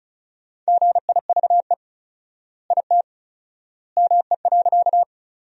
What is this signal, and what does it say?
Morse code, heard as a keyed tone, about 36 wpm, 700 Hz. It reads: GIVE IT ME.
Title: 4x Word-Spacing - Sentences from Top 200 Words - 35wpm